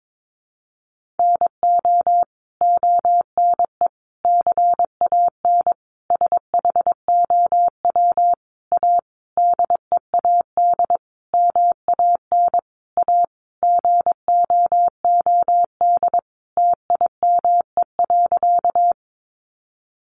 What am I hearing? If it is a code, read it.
Morse code, heard as a keyed tone, about 22 wpm, 700 Hz. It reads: NO ONE CAN SHOW A DEAD MAN A GOOD TIME.
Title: Sentences from Top 1500 Words 22wpm